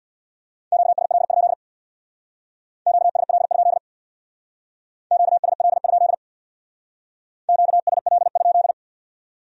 Morse code, heard as a keyed tone, about 64 wpm, 700 Hz. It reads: QSL? QSL? QSL? QSL?